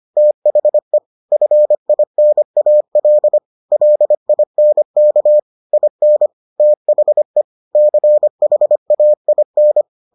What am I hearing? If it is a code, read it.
Morse code, heard as a keyed tone, about 25 wpm, 600 Hz. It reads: THE FINAL LINK IN THE CHAIN